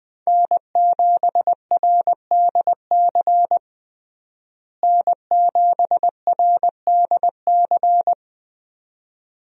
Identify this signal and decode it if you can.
Morse code, heard as a keyed tone, about 20 wpm, 700 Hz. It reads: N7RDC N7RDC
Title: USA Call Signs - 5 Characters - Repeated 2x - 20wpm